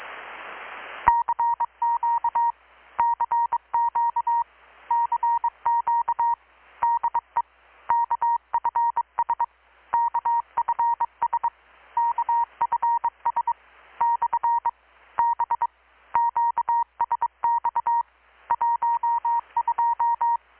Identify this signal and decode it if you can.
Morse code, heard as a keyed tone, about 23 wpm, 940 Hz. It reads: CQ CQ CQ DE KFS KFS KFS / B QSX 12
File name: blog / swl / 6-abril-2019-12695-USB-KPS